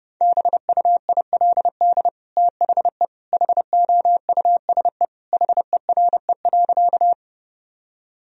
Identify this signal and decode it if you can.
Morse code, heard as a keyed tone, about 30 wpm, 700 Hz. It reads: BUILD THE HOUSE HERE.
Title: Sentences from Top 1000 Words - Review Part 1 - 30wpm